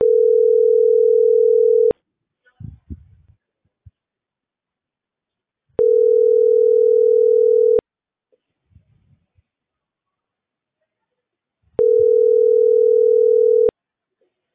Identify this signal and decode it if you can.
Ringback tone (North America), 440 + 480 Hz.